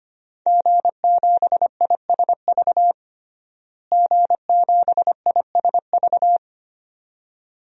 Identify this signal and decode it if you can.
Morse code, heard as a keyed tone, about 25 wpm, 700 Hz. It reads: G7ISV G7ISV
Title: DXCC-100 Call Signs - 5 Characters - Repeated 2x - 25wpm